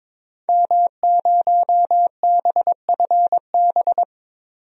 Morse code, heard as a keyed tone, about 22 wpm, 700 Hz. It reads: M0BFB
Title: DXCC-100 Call Signs - 5 Characters - 22wpm